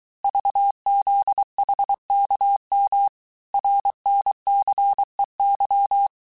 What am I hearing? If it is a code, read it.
Morse code, heard as a keyed tone, about 23 wpm, 800 Hz. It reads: VZHKM RNCEY
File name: cours-du-mercredi / audios / cours-855-texte-chiffre-vitesse-22-mpm